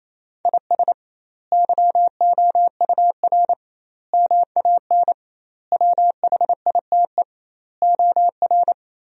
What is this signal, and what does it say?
Morse code, heard as a keyed tone, about 28 wpm, 700 Hz. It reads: IS YOUR MAN WHITE OR
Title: 2x Word-Spacing - Sentences from Top 400 Words - 28wpm